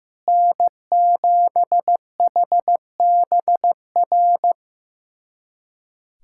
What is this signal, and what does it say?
Morse code, heard as a keyed tone, about 15 wpm, 700 Hz. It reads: N7HBR
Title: USA Callsign DE Callsign - 5 Characters - 15wpm